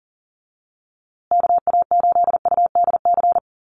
Morse code, heard as a keyed tone, about 40 wpm, 700 Hz. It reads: KA8UDC
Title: USA Call Signs Repeated 2x 40wpm